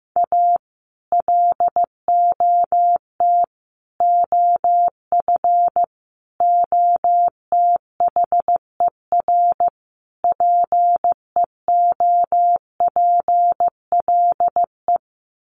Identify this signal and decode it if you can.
Morse code, heard as a keyed tone, about 15 wpm, 700 Hz. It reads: A LOT OF OTHER PEOPLE
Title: Sets of 5 Words Encore2 15wpm